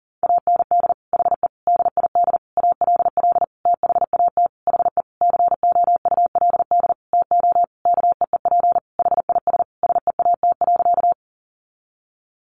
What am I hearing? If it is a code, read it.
Morse code, heard as a keyed tone, about 40 wpm, 700 Hz. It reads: AND HE DID ALL THAT HE COULD TO KEEP HIS SEAT.
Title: Sentences from Top 1500 Words 40wpm